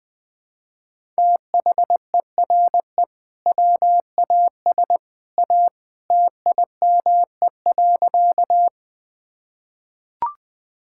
Morse code, heard as a keyed tone, about 21 wpm, 700 Hz. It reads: THERE WAS A TIME.
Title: Sentences with 2-Word Phrases - Spoken Beginning - 20wpm